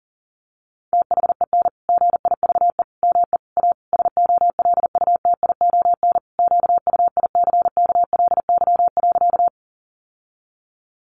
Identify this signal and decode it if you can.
Morse code, heard as a keyed tone, about 40 wpm, 700 Hz. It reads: THEN GIVE ME A SOLUTION QUICKLY.
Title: Sentences from Top 1300 Words 40wpm